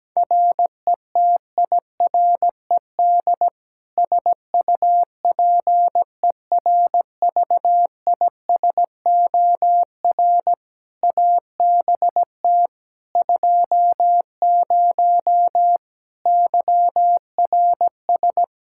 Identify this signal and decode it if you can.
Morse code, heard as a keyed tone, about 17 wpm, 700 Hz. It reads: RETIRED SUPERVISOR ABT 20 YRS
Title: Practice QSO - Exchange 3 - More Info - 17wpm